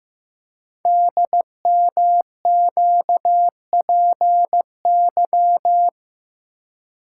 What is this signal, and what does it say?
Morse code, heard as a keyed tone, about 15 wpm, 700 Hz. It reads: DMQPY
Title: Code Groups - 5 Letters - 15wpm